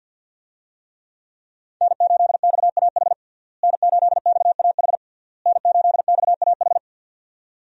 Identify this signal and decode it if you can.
Morse code, heard as a keyed tone, about 51 wpm, 690 Hz. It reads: N8XAH N8XAH N8XAH
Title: USA Call Signs - 5 Characters - Repeated 3x - 50wpm